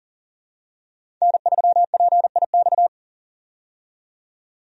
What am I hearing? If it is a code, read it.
Morse code, heard as a keyed tone, about 40 wpm, 700 Hz. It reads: N3PIX
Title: USA Call Signs - 5 Characters - 40wpm